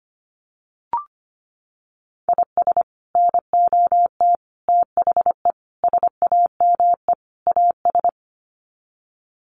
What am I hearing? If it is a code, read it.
Morse code, heard as a keyed tone, about 25 wpm, 700 Hz. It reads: IS NOT THE SAME AS
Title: Sets of 5 Words Encore 25wpm